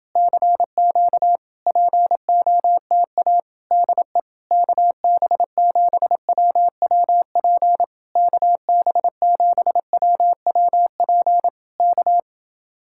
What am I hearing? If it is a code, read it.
Morse code, heard as a keyed tone, about 28 wpm, 700 Hz. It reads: CQ POTA DE KB7WWP KB7WWP K